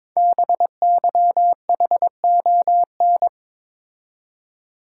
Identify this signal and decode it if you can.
Morse code, heard as a keyed tone, about 22 wpm, 700 Hz. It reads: BYHON
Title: Code Groups - 5 Letters - 22wpm